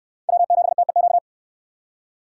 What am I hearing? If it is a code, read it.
Morse code, heard as a keyed tone, about 68 wpm, 700 Hz. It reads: W7TEQ